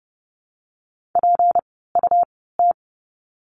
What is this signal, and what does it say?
Morse code, heard as a keyed tone, about 30 wpm, spaced as for 17 wpm, 700 Hz. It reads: PUT
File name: course / 30:10wpm / 237 Lesson 38f - Letter q - Words 30wpm